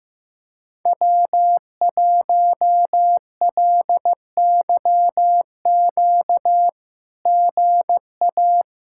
Morse code, heard as a keyed tone, about 15 wpm, 700 Hz. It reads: W1LYQ GA
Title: SOTA QSO Practice 15wpm